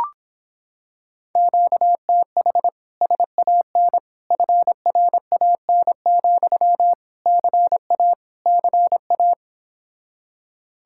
Morse code, heard as a keyed tone, about 26 wpm, 700 Hz. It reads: QTH SAN FRAN, CA CA